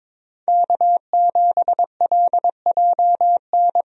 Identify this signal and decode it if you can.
Morse code, heard as a keyed tone, about 22 wpm, 700 Hz. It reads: K7LJN
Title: USA Call Signs - 5 Characters - Repeated 2x - 22wpm